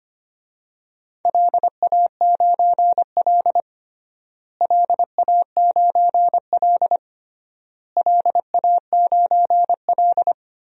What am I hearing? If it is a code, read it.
Morse code, heard as a keyed tone, about 25 wpm, 700 Hz. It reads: LA9L LA9L LA9L
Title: DXCC-100 Call Signs - 4 Characters - Repeated 3x - 25wpm